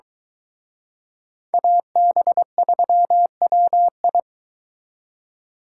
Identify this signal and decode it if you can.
Morse code, heard as a keyed tone, about 23 wpm, 700 Hz. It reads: AB3WI